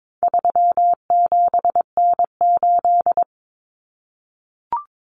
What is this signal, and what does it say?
Morse code, heard as a keyed tone, about 22 wpm, 700 Hz. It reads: 37N8
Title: Common Cut Numbers - 4-Digit 22wpm